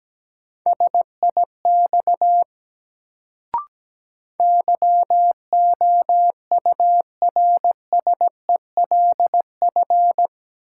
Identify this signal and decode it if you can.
Morse code, heard as a keyed tone, about 17 wpm, 700 Hz. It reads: SIX YOURSELF
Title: Sentences from Top 600 Words - New Word Drill - 17wpm